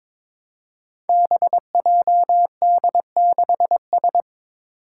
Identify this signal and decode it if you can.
Morse code, heard as a keyed tone, about 22 wpm, 700 Hz. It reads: BJD6S
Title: Five Challenging Letter-Number Combinations 22wpm